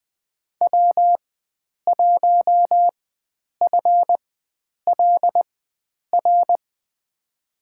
Morse code, heard as a keyed tone, about 21 wpm, spaced as for 9 wpm, 700 Hz. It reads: W1FLR
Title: USA Call Signs - 5 Characters - 4x Character Spacing - 20wpm